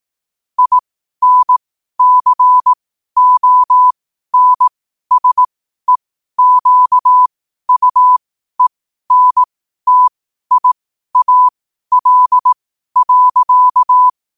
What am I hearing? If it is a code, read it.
Morse code, heard as a keyed tone, about 18 wpm, spaced as for 12 wpm, 1000 Hz. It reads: INCONSEQUENTIAL.
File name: qotdfiles / 12wpm